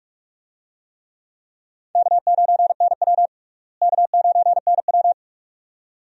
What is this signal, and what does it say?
Morse code, heard as a keyed tone, about 46 wpm, 700 Hz. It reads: K9NW K9NW
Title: USA Call Signs - 4 Characters - Repeated 2x - 45wpm